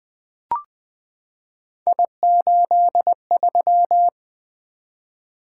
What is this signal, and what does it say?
Morse code, heard as a keyed tone, about 20 wpm, 700 Hz. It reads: I83
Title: Three Letter-Number Combinations - 20wpm